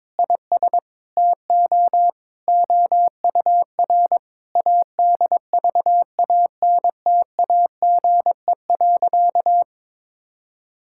Morse code, heard as a keyed tone, about 22 wpm, 700 Hz. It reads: IS TO OUR ADVANTAGE.
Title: Sentences from Top 1000 Words - Review Part 6 - 22wpm